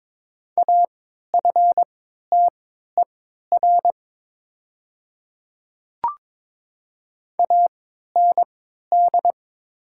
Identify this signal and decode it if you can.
Morse code, heard as a keyed tone, about 22 wpm, spaced as for 12 wpm, 700 Hz. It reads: AFTER AND
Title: 3x Character Spacing - Top 300 Words - 22wpm